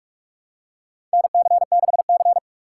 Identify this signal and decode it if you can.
Morse code, heard as a keyed tone, about 47 wpm, 700 Hz. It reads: NC6C
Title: USA Call Signs - 4 Characters - 2x Character Spacing - 45wpm